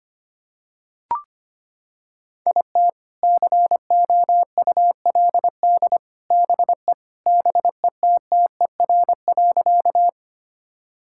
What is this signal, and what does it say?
Morse code, heard as a keyed tone, about 25 wpm, 700 Hz. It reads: IT COULD BE BETTER.
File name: Sentences with 2-Word Phrases - 25wpm